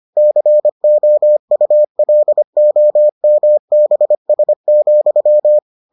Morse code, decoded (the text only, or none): COULOMBS,